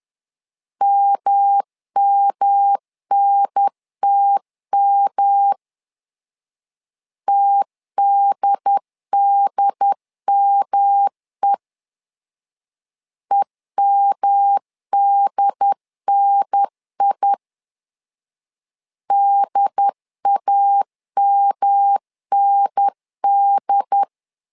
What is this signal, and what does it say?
Morse code, heard as a keyed tone, about 11 wpm, 800 Hz. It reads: MMNTM TDDME EMDNI DAMND
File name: LEZIONI / 3 MWRO / 3 M 10wpm 1828